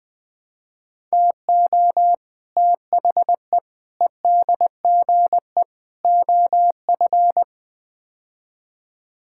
Morse code, heard as a keyed tone, about 20 wpm, 700 Hz. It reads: TO THE EDGE OF
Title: Build A Phrase - 5-Word Phrases - 20wpm